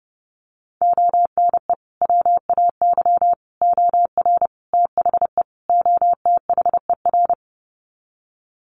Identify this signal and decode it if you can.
Morse code, heard as a keyed tone, about 30 wpm, 700 Hz. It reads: ONE WAY OR THE OTHER
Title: Sets of 5 Words 30wpm